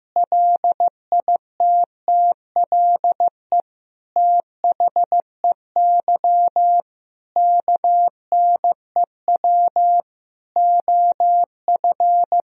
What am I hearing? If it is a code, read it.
Morse code, heard as a keyed tone, about 15 wpm, 700 Hz. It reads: LITTLE THEY KNEW OF
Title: Sentences from Top 1000 Words - Review Part 1 - 15wpm